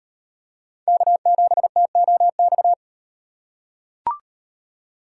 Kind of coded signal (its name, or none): Morse code